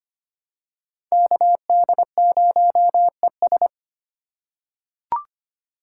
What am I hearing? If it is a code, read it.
Morse code, heard as a keyed tone, about 25 wpm, 700 Hz. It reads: KD0ES
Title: USA Call Signs - 5 Characters - 4x Character Spacing - 25wpm